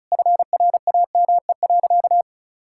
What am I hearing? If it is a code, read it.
Morse code, heard as a keyed tone, about 34 wpm, 700 Hz. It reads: FRAME.
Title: Sentences from Top 1000 Words - Review Part 6 - 35wpm